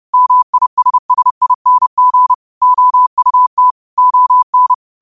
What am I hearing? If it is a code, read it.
Morse code, heard as a keyed tone, about 30 wpm, 1000 Hz. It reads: MISSING OUT ON